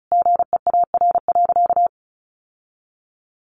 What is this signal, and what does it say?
Morse code, heard as a keyed tone, about 34 wpm, 700 Hz. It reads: GEAR.
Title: Sentences from Top 1400 Words 35wpm